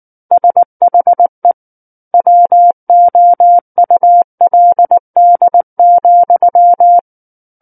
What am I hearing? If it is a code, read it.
Morse code, heard as a keyed tone, about 19 wpm, 700 Hz. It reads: SHE WOULD,
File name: Book One / 10-Chapter 10 - 19 WPM